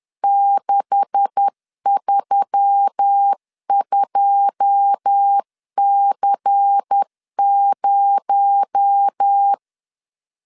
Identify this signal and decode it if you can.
Morse code, heard as a keyed tone, about 10 wpm, 800 Hz. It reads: 632C0
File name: LEZIONI / 9 286 10wpm 2735